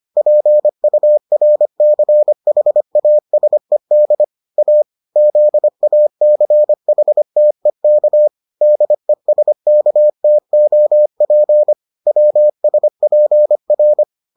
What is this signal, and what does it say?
Morse code, heard as a keyed tone, about 25 wpm, 600 Hz. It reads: PURCHASED A ZACHTEK DESKTOP WSPR